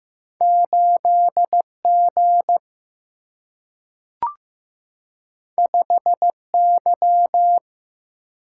Morse code, heard as a keyed tone, about 15 wpm, 700 Hz. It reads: 8 G 5 Y
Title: Single-Letter Single-Number 15wpm